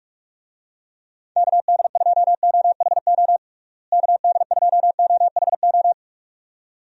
Morse code, heard as a keyed tone, about 46 wpm, 700 Hz. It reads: KD2OHO KD2OHO
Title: USA Call Signs - 6 Characters - Repeated 2x - 45wpm